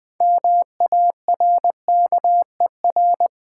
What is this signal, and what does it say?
Morse code, heard as a keyed tone, about 20 wpm, 700 Hz. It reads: MARKER
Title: Build A Word - 3 Letters At A Time - 6-Letter Words - 20wpm